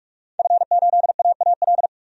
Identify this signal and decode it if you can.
Morse code, heard as a keyed tone, about 46 wpm, 700 Hz. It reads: F8AAL
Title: DXCC-100 Call Signs - 5 Characters - 45wpm